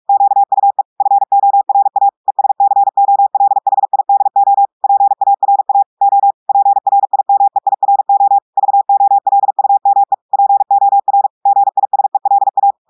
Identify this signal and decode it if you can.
Morse code, heard as a keyed tone, about 46 wpm, 800 Hz. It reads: QUE FORA ESCOLHIDO PARA O PRIMEIRO VOLUME POR GISELA
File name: portugais texte en clair 1 vitesse 45 mots minutes